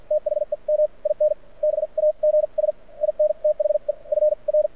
Morse code, heard as a keyed tone, about 50 wpm, spaced as for 41 wpm, 600 Hz. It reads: THEM IN BAGS IN THE FR
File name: archived / CRSnet